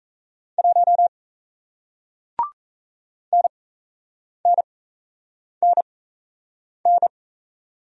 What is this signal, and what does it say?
Morse code, heard as a keyed tone, about 38 wpm, spaced as for 8 wpm, 700 Hz. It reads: 1 NNNN